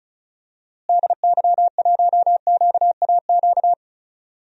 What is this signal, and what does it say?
Morse code, heard as a keyed tone, about 34 wpm, 700 Hz. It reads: DY1QAQ
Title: DXCC-100 Call Signs - 6 Characters - Repeated 2x - 35wpm